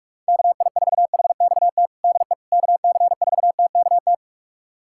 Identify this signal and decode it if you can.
Morse code, heard as a keyed tone, about 46 wpm, 700 Hz. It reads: KI4HXT DE KC4TKT
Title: USA Callsign DE Callsign - 6 Characters - 45wpm